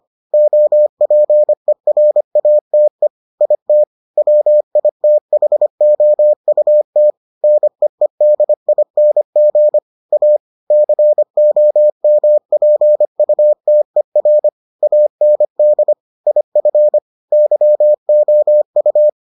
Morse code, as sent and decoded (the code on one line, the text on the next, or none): --- .--. .-. .- - . / .. - / .-- .. - .... --- ..- - / -. . -.. .. -. --. / .- / -.-. --- -- .--. ..- - . .-. / .- -. -.. / .. ..-. / -.-- --- ..-
OPERATE IT WITHOUT NEEDING A COMPUTER AND IF YOU